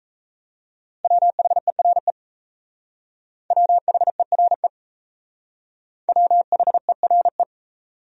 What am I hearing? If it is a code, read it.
Morse code, heard as a keyed tone, about 34 wpm, 700 Hz. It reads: WHERE WHERE WHERE
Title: Speed-Racing Top 300 Words 28wpm